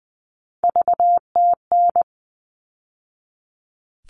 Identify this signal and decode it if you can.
Morse code, heard as a keyed tone, about 20 wpm, 700 Hz. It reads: VTN
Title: All Cut Numbers - 3-Digit 20wpm